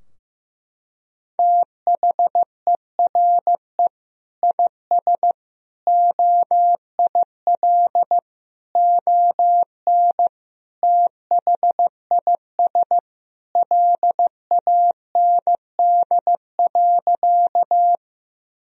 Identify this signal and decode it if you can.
Morse code, heard as a keyed tone, about 15 wpm, 700 Hz. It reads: THERE IS OIL ON THIS LAND.
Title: Sentences from Top 1000 Words - Review Part 4 - 15wpm